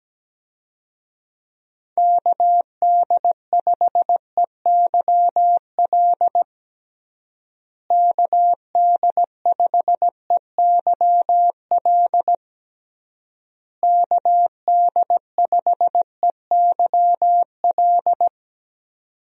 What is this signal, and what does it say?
Morse code, heard as a keyed tone, about 17 wpm, 700 Hz. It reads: KD5EYL KD5EYL KD5EYL
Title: USA Call Signs - 6 Characters - Repeated 3x - 17wpm